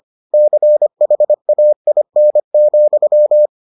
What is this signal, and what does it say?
Morse code, heard as a keyed tone, about 25 wpm, 600 Hz. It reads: CHAIN,